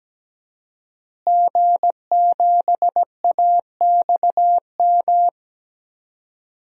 Morse code, heard as a keyed tone, about 17 wpm, 700 Hz. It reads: G7AXM
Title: DXCC-100 Call Signs - 5 Characters - Repeated 2x - 17wpm